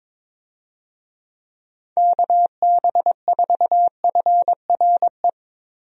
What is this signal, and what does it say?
Morse code, heard as a keyed tone, about 23 wpm, 700 Hz. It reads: KB4FRE
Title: USA Call Signs - 6 Characters - 22wpm